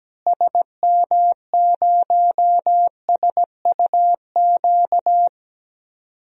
Morse code, heard as a keyed tone, about 17 wpm, 700 Hz. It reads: SM0SUQ
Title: DXCC-100 Call Signs - 6 Characters - Repeated 3x - 17wpm